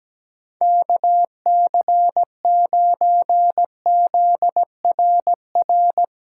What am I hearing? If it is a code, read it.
Morse code, heard as a keyed tone, about 17 wpm, 700 Hz. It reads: KC9ZRR
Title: USA Callsign DE Callsign 17wpm